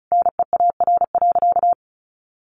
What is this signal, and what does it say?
Morse code, heard as a keyed tone, about 34 wpm, 700 Hz. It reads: NEAR.
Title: Sentences from Top 1500 Words 35wpm